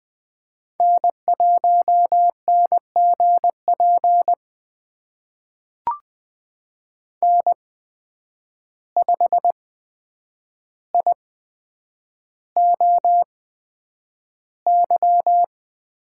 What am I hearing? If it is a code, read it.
Morse code, heard as a keyed tone, about 20 wpm, spaced as for 9 wpm, 700 Hz. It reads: N1NGP N5IOY
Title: USA Call Signs - 5 Characters - 8x Character Spacing - 20wpm